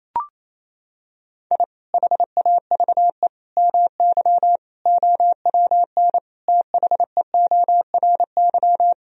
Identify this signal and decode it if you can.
Morse code, heard as a keyed tone, about 28 wpm, 700 Hz. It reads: I HAVE MY OWN THEORY
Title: Sentences from Top 1000 Words - Review Part 3 - 28wpm